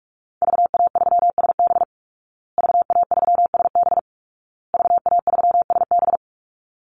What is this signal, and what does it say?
Morse code, heard as a keyed tone, about 44 wpm, 700 Hz. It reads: VA3SB VA3SB VA3SB
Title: Call Signs Repeated 2x - 45wpm